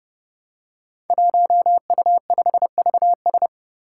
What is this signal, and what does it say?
Morse code, heard as a keyed tone, about 30 wpm, 700 Hz. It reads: 1U5VS
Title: Five Challenging Letter-Number Combinations 30wpm